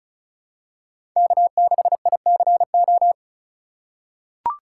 Morse code, heard as a keyed tone, about 36 wpm, 700 Hz. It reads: K6ICO
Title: USA Call Signs - 5 Characters - 2x Character Spacing - 35wpm